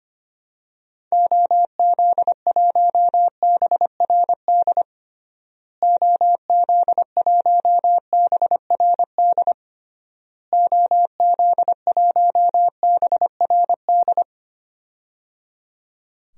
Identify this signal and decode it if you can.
Morse code, heard as a keyed tone, about 25 wpm, 700 Hz. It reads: OZ1BRD OZ1BRD OZ1BRD
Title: DXCC-100 Call Signs - 6 Characters - Repeated 3x - 25wpm